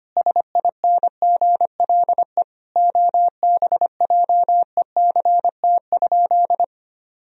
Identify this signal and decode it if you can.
Morse code, heard as a keyed tone, about 25 wpm, 700 Hz. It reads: SINGLE OBJECT?
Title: Sentences from Top 1000 Words - Review Part 4 - 25wpm